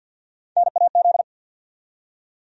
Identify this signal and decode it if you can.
Morse code, heard as a keyed tone, about 47 wpm, 700 Hz. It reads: NAZ
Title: USA Call Sign Postfixes - 3 Letters 50wpm